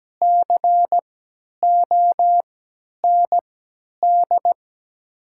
Morse code, heard as a keyed tone, about 17 wpm, spaced as for 10 wpm, 700 Hz. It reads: COND
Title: 3x Character Spacing - Top 300 Words - 17wpm